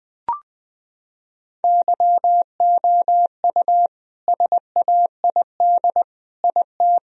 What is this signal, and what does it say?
Morse code, heard as a keyed tone, about 20 wpm, 700 Hz. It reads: YOU SAID IT
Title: Sentences from Top 1000 Words - Review Part 2 - 20wpm